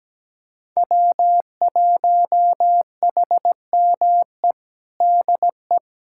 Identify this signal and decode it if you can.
Morse code, heard as a keyed tone, about 17 wpm, 700 Hz. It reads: W1HME DE
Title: USA Callsign DE Callsign - 5 Characters - 17wpm